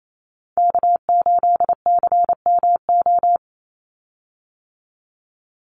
Morse code, heard as a keyed tone, about 28 wpm, 700 Hz. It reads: K8CMO
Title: Call Signs - 28wpm